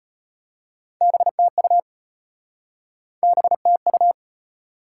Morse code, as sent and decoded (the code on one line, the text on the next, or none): -... - ..- / -... - ..-
BTU BTU